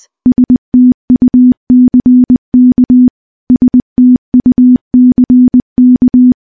Morse code, heard as a keyed tone, about 20 wpm, 250 Hz. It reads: STUCK STUCK
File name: morse / morse-training-20WPM